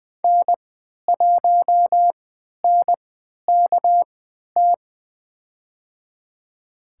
Morse code, heard as a keyed tone, about 20 wpm, spaced as for 11 wpm, 700 Hz. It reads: N1NKT
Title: USA Call Signs - 3x Character Spacing - 20wpm